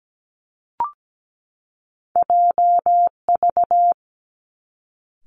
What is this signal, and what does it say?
Morse code, heard as a keyed tone, about 17 wpm, 700 Hz. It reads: J V
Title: Two Challenging Letter-Number Combinations 17wpm